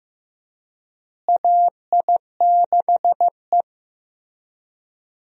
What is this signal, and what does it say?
Morse code, heard as a keyed tone, about 15 wpm, 700 Hz. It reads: AI6E
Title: USA Call Signs - 4 Characters - Repeated 3x - 15wpm